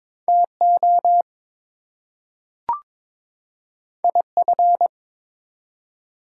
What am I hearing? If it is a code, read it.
Morse code, heard as a keyed tone, about 23 wpm, 700 Hz. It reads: TO IF
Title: Two Letter Words - 22wpm